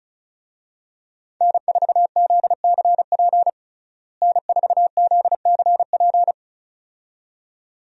Morse code, heard as a keyed tone, about 36 wpm, 700 Hz. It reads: N4ZCP N4ZCP
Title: USA Call Signs - 5 Characters - Repeated 2x - 35wpm